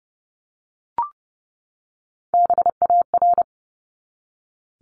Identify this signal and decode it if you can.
Morse code, heard as a keyed tone, about 30 wpm, 700 Hz. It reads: BAR
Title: Top 1751-2000 Words 30wpm